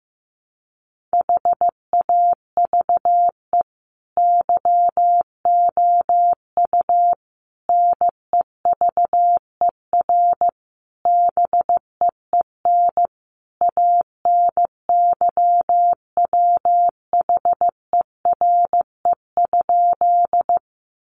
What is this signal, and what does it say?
Morse code, heard as a keyed tone, about 15 wpm, 700 Hz. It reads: HAVE YOU NEVER BEEN ANYWHERE?
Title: Sentences from Top 1400 Words 15wpm